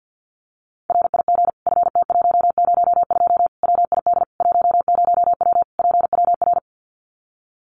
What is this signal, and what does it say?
Morse code, heard as a keyed tone, about 50 wpm, 700 Hz. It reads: RIG FT102 WID 10W PWR
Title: Common QSO Phrases 50wpm